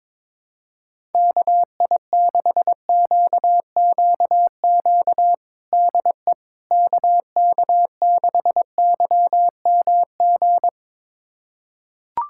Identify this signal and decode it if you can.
Morse code, heard as a keyed tone, about 22 wpm, 700 Hz. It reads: KI6QQQ DE KK6YMG
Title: USA Callsign DE Callsign 22wpm